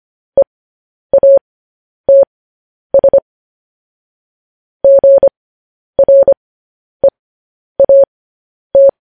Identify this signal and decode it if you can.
Morse code, heard as a keyed tone, about 25 wpm, spaced as for 10 wpm, 550 Hz. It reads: EATS GREAT